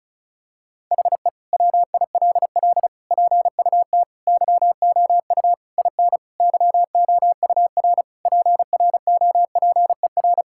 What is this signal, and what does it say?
Morse code, heard as a keyed tone, about 34 wpm, 700 Hz. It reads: HE WILL PUT YOU IN YOUR PROPER
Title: Sentences from Top 1000 Words - Review Part 5 - 35wpm